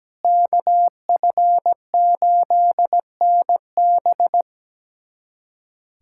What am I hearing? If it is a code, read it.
Morse code, heard as a keyed tone, about 17 wpm, 700 Hz. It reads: KF8NB
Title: USA Callsign DE Callsign - 5 Characters - 17wpm